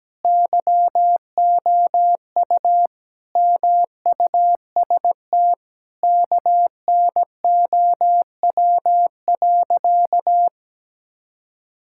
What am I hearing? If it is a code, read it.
Morse code, heard as a keyed tone, about 17 wpm, 700 Hz. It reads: YOU MUST KNOW.
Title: Sentences from Top 1000 Words - Review Part 6 - 17wpm